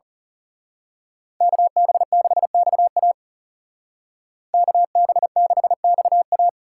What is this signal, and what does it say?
Morse code, heard as a keyed tone, about 35 wpm, 700 Hz. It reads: KB6XA KB6XA
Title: Speed Racing - USA 2x2 Call Signs 30wpm